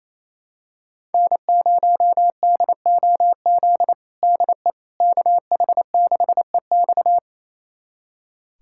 Morse code, heard as a keyed tone, about 27 wpm, 700 Hz. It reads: N0DOZ DE KH6EX
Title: USA Callsign DE Callsign - 5 Characters - 28wpm